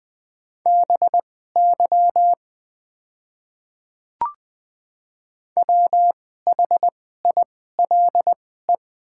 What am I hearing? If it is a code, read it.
Morse code, heard as a keyed tone, about 20 wpm, spaced as for 14 wpm, 690 Hz. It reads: BY WHILE
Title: 2x Character Spacing - Top 500 Words - 20wpm